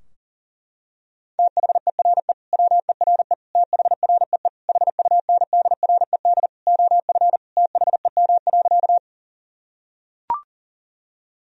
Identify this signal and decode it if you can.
Morse code, heard as a keyed tone, about 37 wpm, 700 Hz. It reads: THERE WERE THREE HUNDRED OF THEM.
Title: Sentences from Top 1000 Words - Review Part 2 - 40wpm